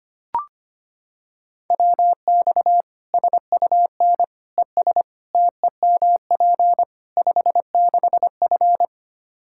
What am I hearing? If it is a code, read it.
Morse code, heard as a keyed tone, about 25 wpm, 700 Hz. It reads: WX SUN ES TEMP 56F
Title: Practice QSO - Exchange 2 - Info - 25wpm